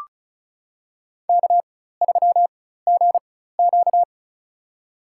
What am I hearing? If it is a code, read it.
Morse code, heard as a keyed tone, about 36 wpm, spaced as for 16 wpm, 700 Hz. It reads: K3GQ
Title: USA Call Signs - 4 Characters - 4x Character Spacing - 35wpm